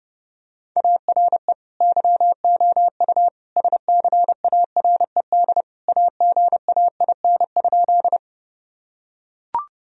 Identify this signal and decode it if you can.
Morse code, heard as a keyed tone, about 30 wpm, 700 Hz. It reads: ARE YOU SCARED AGAIN?
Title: Sentences from Top 1000 Words - Review Part 4 - 30wpm